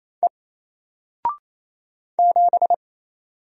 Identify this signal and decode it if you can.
Morse code, heard as a keyed tone, about 28 wpm, 700 Hz. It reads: E 7
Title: Single Letter-Number Alt 28wpm